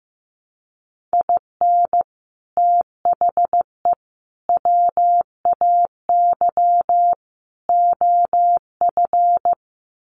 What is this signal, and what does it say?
Morse code, heard as a keyed tone, about 15 wpm, 700 Hz. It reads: IN THE WAY OF
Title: Sets of 4 Words 15wpm